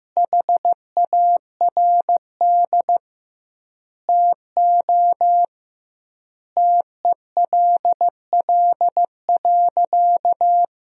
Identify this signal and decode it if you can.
Morse code, heard as a keyed tone, about 15 wpm, 700 Hz. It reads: HARD TO TELL.
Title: 2x Word-Spacing - Sentences from Top 300 Words - 15wpm